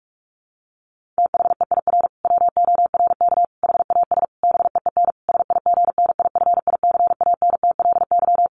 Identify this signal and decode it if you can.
Morse code, heard as a keyed tone, about 44 wpm, 700 Hz. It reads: THEIR WORK HAS BEEN SIGNIFICANTLY
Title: Sentences from Top 1500 Words 45wpm